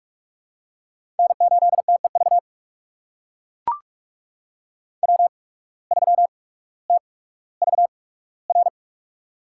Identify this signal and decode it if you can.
Morse code, heard as a keyed tone, about 44 wpm, spaced as for 19 wpm, 700 Hz. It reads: N8TEV W3TVR
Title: USA Call Signs - 5 Characters - 8x Character Spacing - 45wpm